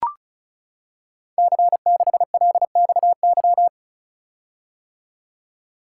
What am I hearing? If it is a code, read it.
Morse code, heard as a keyed tone, about 36 wpm, 700 Hz. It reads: C6LXY
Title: Code Groups - 5 Letter:Numbers - 35wpm